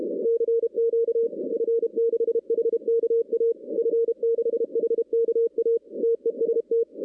Morse code, heard as a keyed tone, about 30 wpm, 470 Hz. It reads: CQ F6HKA F6HKA TEST
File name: 2016 arrl dx cw / 2016-ARRL-DX-CW-K5TR- - 0-31